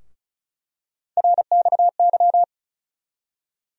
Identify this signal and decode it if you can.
Morse code, heard as a keyed tone, about 36 wpm, 700 Hz. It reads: RXY
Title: USA Call Sign Postfixes - 3 Letters 35wpm